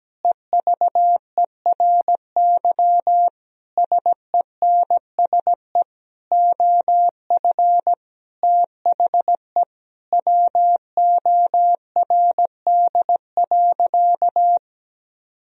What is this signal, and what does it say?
Morse code, heard as a keyed tone, about 17 wpm, 700 Hz. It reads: EVERY SENSE OF THE WORD.
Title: Sentences from Top 1000 Words - Review Part 4 - 17wpm